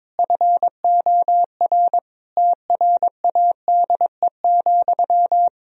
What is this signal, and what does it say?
Morse code, heard as a keyed tone, about 22 wpm, 700 Hz. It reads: FOR TRADE,
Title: Sentences from Top 1000 Words - Review Part 4 - 22wpm